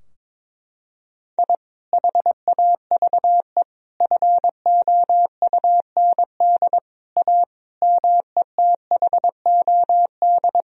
Morse code, heard as a keyed tone, about 22 wpm, 700 Hz. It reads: I HAVE FOUND A METHOD
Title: Sentences from Top 1000 Words - Review Part 3 - 22wpm